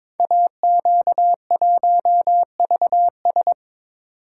Morse code, heard as a keyed tone, about 22 wpm, 700 Hz. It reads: AQ1VS
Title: Code Groups - 5 Letter:Numbers - 22wpm